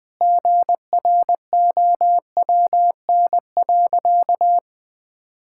Morse code, heard as a keyed tone, about 20 wpm, 700 Hz. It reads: GROWN.
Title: Sentences with 4-Word Phrases - Spoken Beginning - 20wpm